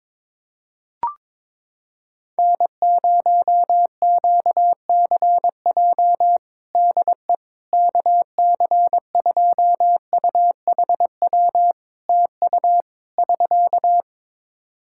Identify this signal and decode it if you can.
Morse code, heard as a keyed tone, about 22 wpm, 700 Hz. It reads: N0QCJ DE KC2UHW TU <SK>
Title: K1USN SST QSO Practice 22wpm